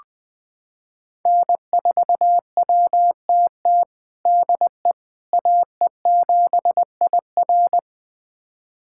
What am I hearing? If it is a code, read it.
Morse code, heard as a keyed tone, about 20 wpm, 700 Hz. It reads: N4WTT DE AE7IR
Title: USA Callsign DE Callsign - 5 Characters - 20wpm